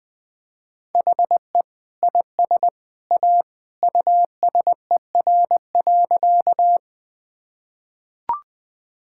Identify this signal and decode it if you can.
Morse code, heard as a keyed tone, about 20 wpm, 700 Hz. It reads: HE IS A USER.
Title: Sentences from Top 1000 Words - Review Part 5 - 20wpm